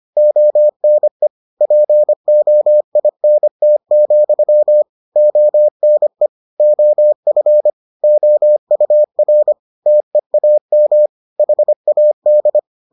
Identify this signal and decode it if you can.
Morse code, heard as a keyed tone, about 25 wpm, 600 Hz. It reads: ONE POINT, ONE OF OUR TEAM HAD